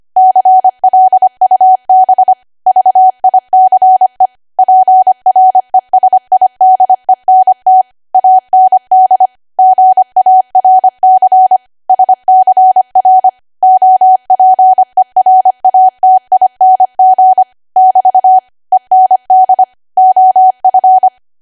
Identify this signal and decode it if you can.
Morse code, heard as a keyed tone, about 25 wpm, 750 Hz. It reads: CLUB VICE PRESIDENT AND GARC SCR OPERATING <BT> END OF
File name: Archive / 25 WPM / 25WPM